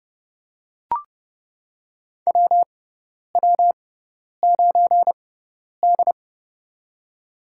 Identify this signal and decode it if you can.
Morse code, heard as a keyed tone, about 29 wpm, spaced as for 10 wpm, 700 Hz. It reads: WW9D
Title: USA Call Signs - 4 Characters - 6x Character Spacing - 30wpm